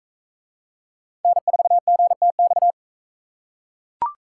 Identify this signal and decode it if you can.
Morse code, heard as a keyed tone, about 42 wpm, 700 Hz. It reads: N4GTX